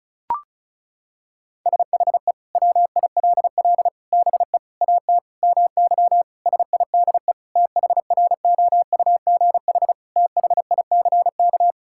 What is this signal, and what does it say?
Morse code, heard as a keyed tone, about 36 wpm, 700 Hz. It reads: SHE WILL BE AT MY SIDE THROUGH THICK
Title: Sentences from Top 1600 Words 35wpm